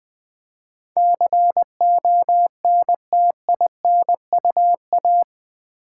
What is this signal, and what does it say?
Morse code, heard as a keyed tone, about 20 wpm, 700 Hz. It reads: CONTINUA
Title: Build A Word - 4 Letters At A Time - 11-Letter Words - 20wpm